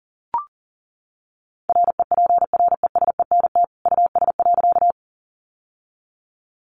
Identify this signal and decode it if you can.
Morse code, heard as a keyed tone, about 40 wpm, 700 Hz. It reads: REPRESENT US.
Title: Sentences from Top 1200 Words 40wpm